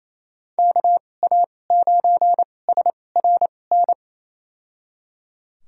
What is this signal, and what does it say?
Morse code, heard as a keyed tone, about 28 wpm, spaced as for 20 wpm, 700 Hz. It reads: KA9SRN
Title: USA Call Signs - 6 Characters - 2x Character Spacing - 28wpm